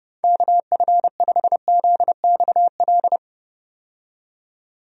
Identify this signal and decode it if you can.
Morse code, heard as a keyed tone, about 30 wpm, 700 Hz. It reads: KF5ZXL
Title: USA Call Signs - 6 Characters - 30wpm